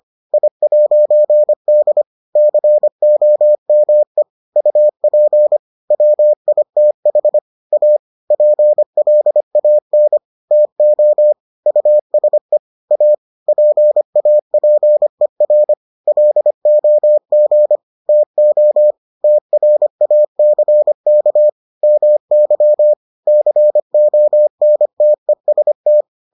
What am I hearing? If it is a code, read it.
Morse code, heard as a keyed tone, about 25 wpm, 600 Hz. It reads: I'D COME UP WITH A PLAN TO USE A PAPER LOG TO TRACK MY CONTEST